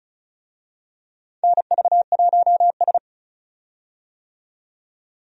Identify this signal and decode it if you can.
Morse code, heard as a keyed tone, about 36 wpm, 700 Hz. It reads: NV1S
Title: USA Call Signs - 4 Characters - 35wpm